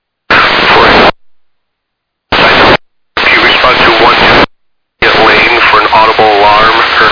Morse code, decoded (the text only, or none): none